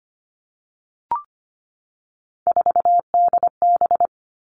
Morse code, heard as a keyed tone, about 25 wpm, 700 Hz. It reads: 4DB
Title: All Cut Numbers - 3-Digit 25wpm